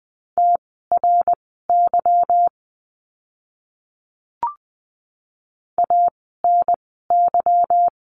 Morse code, heard as a keyed tone, about 20 wpm, spaced as for 14 wpm, 700 Hz. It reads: TRY ANY
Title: 230 Lesson 37f - Letter x - Words 20wpm